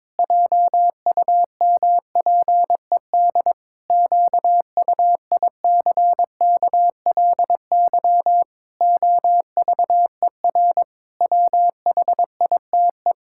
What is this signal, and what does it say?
Morse code, heard as a keyed tone, about 22 wpm, 700 Hz. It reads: JUMPED QUICKLY OVER WHITE
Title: Pangram Challenge v2 - 22wpm